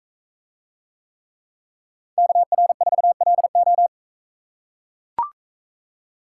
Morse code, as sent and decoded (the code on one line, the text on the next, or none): -.- .-. ....- .-.. ---
KR4LO